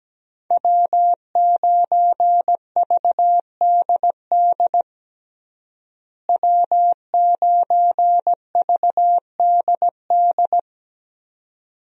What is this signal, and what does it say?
Morse code, heard as a keyed tone, about 17 wpm, 700 Hz. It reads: W9VDD W9VDD
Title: USA Call Signs - 5 Characters - Repeated 2x - 17wpm